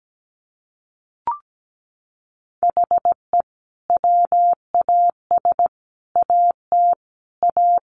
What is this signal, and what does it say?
Morse code, heard as a keyed tone, about 17 wpm, 700 Hz. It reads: HE WAS AT A
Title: Sentences from Top 1400 Words 17wpm